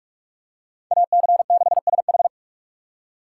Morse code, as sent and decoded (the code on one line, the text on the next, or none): .- -.-. -.... ... ....
AC6SH